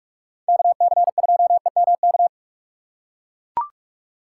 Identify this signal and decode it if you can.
Morse code, heard as a keyed tone, about 42 wpm, 700 Hz. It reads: KC2EMK
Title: USA Call Signs - 6x Character Spacing - 45wpm